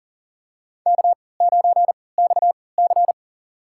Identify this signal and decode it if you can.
Morse code, heard as a keyed tone, about 40 wpm, spaced as for 23 wpm, 700 Hz. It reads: K9XC
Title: USA Call Signs - 4 Characters - 3x Character Spacing - 40wpm